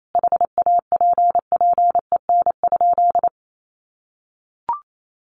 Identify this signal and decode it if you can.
Morse code, heard as a keyed tone, about 28 wpm, 700 Hz. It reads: HAPPEN?
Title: Sentences from Top 400 Words 28wpm